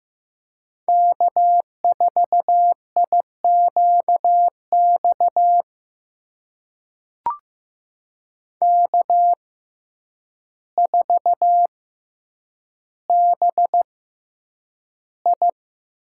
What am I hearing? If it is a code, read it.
Morse code, heard as a keyed tone, about 15 wpm, spaced as for 8 wpm, 700 Hz. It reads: K4IQX K4BI
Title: USA Call Signs - 5 Characters - 6x Character Spacing - 15wpm